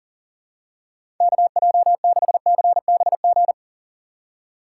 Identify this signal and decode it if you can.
Morse code, heard as a keyed tone, about 40 wpm, 700 Hz. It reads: KJ6CBG
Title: USA Call Signs - 6 Characters - 3x Character Spacing - 40wpm